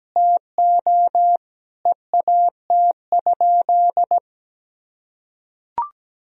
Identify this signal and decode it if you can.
Morse code, heard as a keyed tone, about 17 wpm, 700 Hz. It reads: TO EAT?
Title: Sentences from Top 300 Words and Abbreviations 17wpm